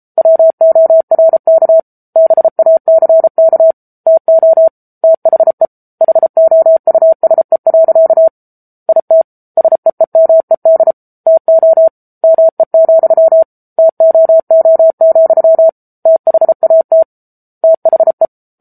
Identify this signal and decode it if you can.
Morse code, heard as a keyed tone, about 33 wpm, 650 Hz. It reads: WORK BACK TO THE HOUSE. IT SEEMED TO ME, TOO, THAT THE
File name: Book Two / 07-Chapter 7 - 33 WPM